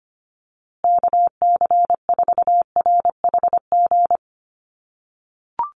Morse code, heard as a keyed tone, about 25 wpm, 700 Hz. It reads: KC4RHG
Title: USA Call Signs Repeated 2x 25wpm